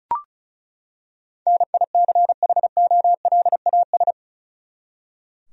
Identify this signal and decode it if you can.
Morse code, heard as a keyed tone, about 36 wpm, 700 Hz. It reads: NICHOLAS
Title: USA Common Names - 35wpm